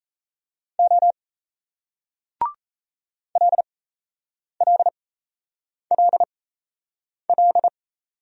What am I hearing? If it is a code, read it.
Morse code, heard as a keyed tone, about 38 wpm, spaced as for 8 wpm, 700 Hz. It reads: O LLLL